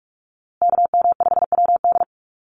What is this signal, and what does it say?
Morse code, heard as a keyed tone, about 44 wpm, 700 Hz. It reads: KM5WD
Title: USA Call Signs Repeated 2x 45wpm